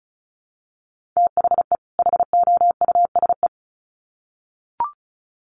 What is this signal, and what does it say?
Morse code, heard as a keyed tone, about 34 wpm, 700 Hz. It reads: THE HOUSE
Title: Sets of 2 Words 35wpm